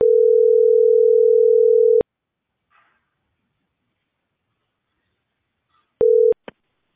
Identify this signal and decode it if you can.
Ringback tone (North America), 440 + 480 Hz.